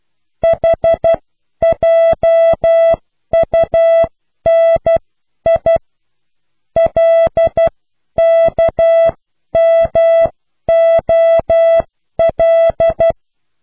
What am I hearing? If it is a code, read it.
Morse code, heard as a keyed tone, about 12 wpm, spaced as for 10 wpm, 660 Hz. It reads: HJUNI LKMOL